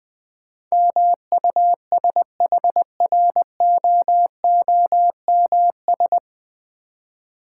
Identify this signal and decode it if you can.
Morse code, heard as a keyed tone, about 20 wpm, 700 Hz. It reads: MUSHROOMS
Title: Difficult Words with Hints - 20wpm